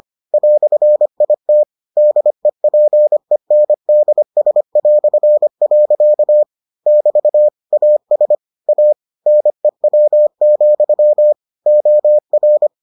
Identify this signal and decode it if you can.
Morse code, heard as a keyed tone, about 25 wpm, 600 Hz. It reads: "IT DEPENDS". <BT> AS A NEW, OR